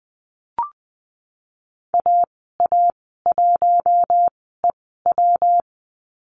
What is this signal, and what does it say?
Morse code, heard as a keyed tone, about 20 wpm, spaced as for 14 wpm, 700 Hz. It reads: AA1EW
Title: 122 Lesson 20g - Number 3 - Call Signs 20wpm